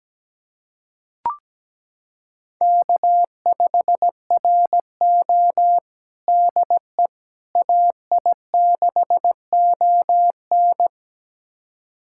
Morse code, heard as a keyed tone, about 17 wpm, 700 Hz. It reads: K5RO DE AI6ON
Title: USA Callsign DE Callsign 17wpm